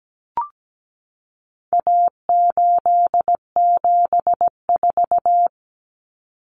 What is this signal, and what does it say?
Morse code, heard as a keyed tone, about 17 wpm, 700 Hz. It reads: A874
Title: Common Cut Numbers - 4-Digit 17wpm